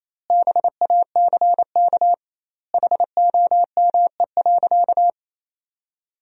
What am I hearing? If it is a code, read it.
Morse code, heard as a keyed tone, about 28 wpm, 700 Hz. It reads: BACK HOME.
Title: 2x Word-Spacing - Sentences from Top 500 Words - 28wpm